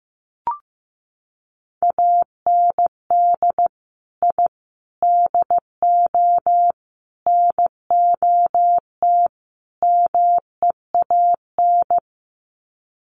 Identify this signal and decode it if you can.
Morse code, heard as a keyed tone, about 15 wpm, 700 Hz. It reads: AND I DO NOT MEAN
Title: Sets of 5 Words Encore2 15wpm